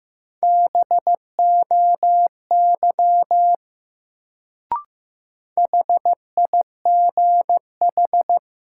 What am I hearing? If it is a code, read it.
Morse code, heard as a keyed tone, about 15 wpm, 700 Hz. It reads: BOY HIGH
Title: Sentences from Top 200 Words - New Word Drill - 15wpm